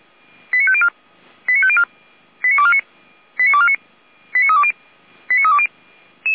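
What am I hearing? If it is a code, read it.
Five-tone selective call (ZVEI), heard as quick runs of five tones, tones of about 70 ms. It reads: 89504 89504 89249 89249 89320 89320
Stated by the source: ZVEI five-tone selective call received by SDR.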